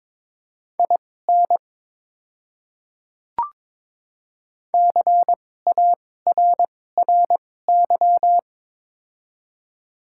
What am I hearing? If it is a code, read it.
Morse code, heard as a keyed tone, about 21 wpm, spaced as for 16 wpm, 700 Hz. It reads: IN CARRY
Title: 2x Character Spacing - Top 300 Words - 22wpm